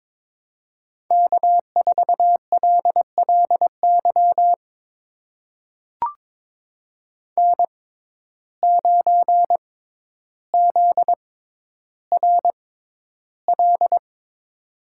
Morse code, heard as a keyed tone, about 22 wpm, spaced as for 11 wpm, 700 Hz. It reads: K4LLY N9ZRL
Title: USA Call Signs - 6x Character Spacing - 22wpm